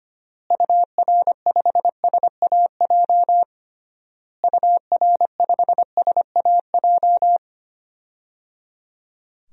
Morse code, heard as a keyed tone, about 25 wpm, 700 Hz. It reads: UR5SAJ UR5SAJ
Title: DXCC-100 Call Signs - 6 Characters - Repeated 3x - 25wpm